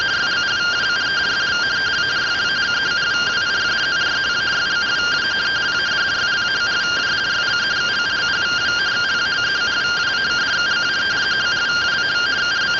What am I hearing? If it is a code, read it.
Two-tone teleprinter signal, about 170 Hz shift.
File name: findings / CT 1 Scout 0001.gen-06 rtty rx